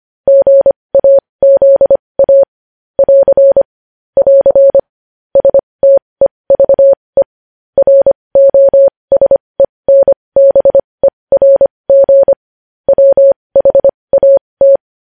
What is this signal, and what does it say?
Morse code, heard as a keyed tone, about 25 wpm, spaced as for 20 wpm, 550 Hz. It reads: GAZA <AR> <AR> STEVE ROSENBERG WHAT
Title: News Headlines at 20